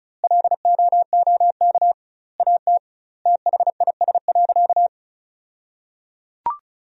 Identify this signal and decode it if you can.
Morse code, heard as a keyed tone, about 34 wpm, 700 Hz. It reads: LOOK AT THIS.
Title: 2x Word-Spacing - Sentences from Top 300 Words - 35wpm